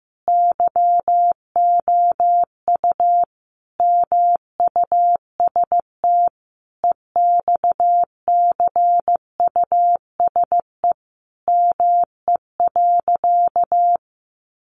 Morse code, heard as a keyed tone, about 15 wpm, 700 Hz. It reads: YOU MUST EXCUSE ME.
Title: Sentences from Top 700 Words 15wpm